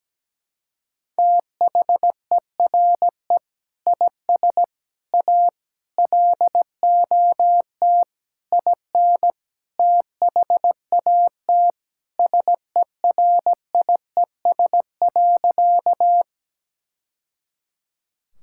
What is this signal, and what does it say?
Morse code, heard as a keyed tone, about 17 wpm, 700 Hz. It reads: THERE IS A LOT IN THAT SERIES.
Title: Sentences from Top 1000 Words - Review Part 5 - 17wpm